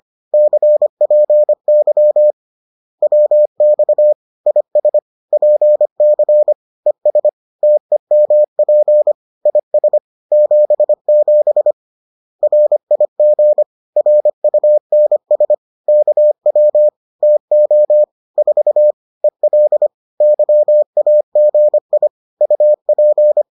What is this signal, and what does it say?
Morse code, heard as a keyed tone, about 25 wpm, 600 Hz. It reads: CPY WX IS PC ES TEMP IS 77 RIG RUNS KW TO 4 EL YAGI UP